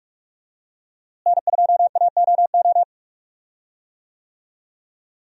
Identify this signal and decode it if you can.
Morse code, heard as a keyed tone, about 46 wpm, 700 Hz. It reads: N2AOO
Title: USA Call Signs - 5 Characters - Repeated 2x - 45wpm